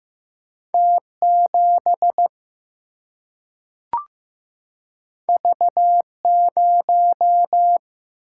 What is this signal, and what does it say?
Morse code, heard as a keyed tone, about 15 wpm, 700 Hz. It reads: T 7 V 0
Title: Single-Letter Single-Number 15wpm